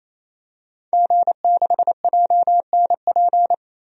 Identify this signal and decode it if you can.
Morse code, heard as a keyed tone, about 28 wpm, 700 Hz. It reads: G6JNP
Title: DXCC-100 Call Signs - 5 Characters - 28wpm